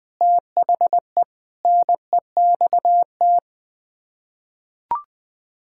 Morse code, heard as a keyed tone, about 20 wpm, 700 Hz. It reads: THE NEXT
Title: Sets of 6 Words - 20wpm